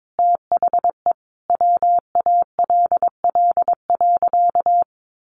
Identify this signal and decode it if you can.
Morse code, heard as a keyed tone, about 22 wpm, 700 Hz. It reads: THE WALL.
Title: Sentences from Top 1100 Words 22wpm